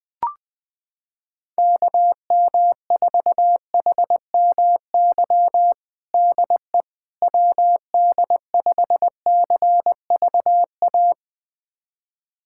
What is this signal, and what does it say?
Morse code, heard as a keyed tone, about 21 wpm, 700 Hz. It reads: KM4HMY DE WD5CVA
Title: USA Callsign DE Callsign - 6 Characters - 20wpm